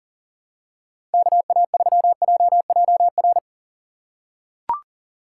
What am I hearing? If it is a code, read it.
Morse code, heard as a keyed tone, about 40 wpm, 690 Hz. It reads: KA3JJR
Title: USA Call Signs - 4x Character Spacing - 40wpm